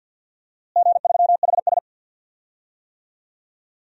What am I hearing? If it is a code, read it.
Morse code, heard as a keyed tone, about 53 wpm, 700 Hz. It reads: G3HS
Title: DXCC-100 Call Signs - 4 Characters - Repeated 3x - 50wpm